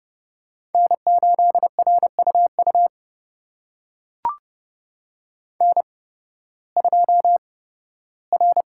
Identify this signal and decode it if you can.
Morse code, heard as a keyed tone, about 30 wpm, spaced as for 16 wpm, 700 Hz. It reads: N8RUU N2R
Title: USA Call Signs - 5 Characters - 8x Character Spacing - 30wpm